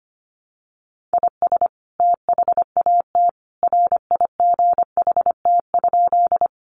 Morse code, heard as a keyed tone, about 25 wpm, 700 Hz. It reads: IS THAT RIGHT?
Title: Sentences from Top 200 Words 25wpm